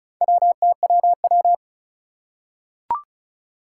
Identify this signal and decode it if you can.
Morse code, heard as a keyed tone, about 36 wpm, 700 Hz. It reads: WTWW
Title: Four Challenging Letter-Number Combinations 35wpm